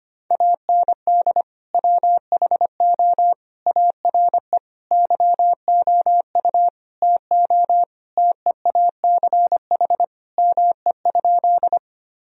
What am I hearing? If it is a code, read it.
Morse code, heard as a keyed tone, about 25 wpm, 700 Hz. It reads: AND WHO ARE YOU TO TEACH ME?
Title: Sentences from Top 1000 Words - Review Part 2 - 25wpm